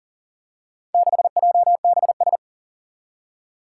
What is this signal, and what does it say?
Morse code, heard as a keyed tone, about 38 wpm, 700 Hz. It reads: 6JBS
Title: Four Challenging Letter-Number Combinations 40wpm